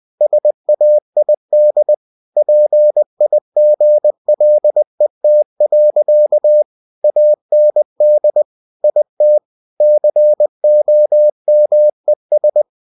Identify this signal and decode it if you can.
Morse code, heard as a keyed tone, about 20 wpm, 600 Hz. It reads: SAID PIGLET. AND IT COMES
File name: cw-books / winnie-the-pooh / winnie-20wpm-page-0058